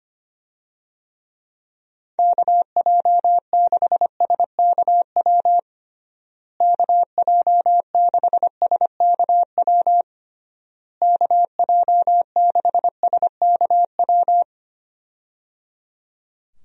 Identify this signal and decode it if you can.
Morse code, heard as a keyed tone, about 25 wpm, 700 Hz. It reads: KJ6SKW KJ6SKW KJ6SKW
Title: USA Call Signs - 6 Characters - Repeated 3x - 25wpm